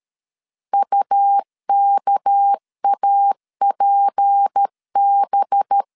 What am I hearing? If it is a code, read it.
Morse code, heard as a keyed tone, about 13 wpm, 800 Hz. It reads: UKAPB
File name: CW / DET / 1 6 1500C S B 12wpm 3431